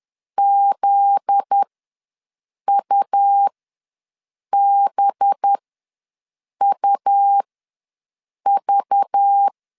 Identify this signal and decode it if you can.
Morse code, heard as a keyed tone, about 10 wpm, spaced as for 6 wpm, 800 Hz. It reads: ZUBUV
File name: LEZIONI / 7 GF9Z / 7 GF9Z ws 10wpm 3919